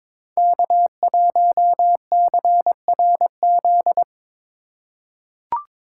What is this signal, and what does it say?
Morse code, heard as a keyed tone, about 23 wpm, 700 Hz. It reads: K1CRZ
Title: USA Call Signs - 3x Character Spacing - 22wpm